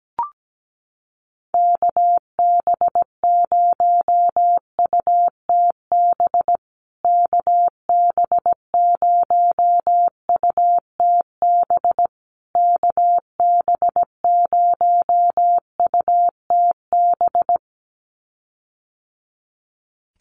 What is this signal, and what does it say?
Morse code, heard as a keyed tone, about 17 wpm, 700 Hz. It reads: KB0UTB KB0UTB KB0UTB
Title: USA Call Signs Repeated 3x 17wpm